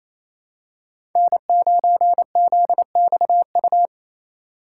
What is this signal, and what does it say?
Morse code, heard as a keyed tone, about 28 wpm, 700 Hz. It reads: N9ZXU
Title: USA Call Signs - 3x Character Spacing - 28wpm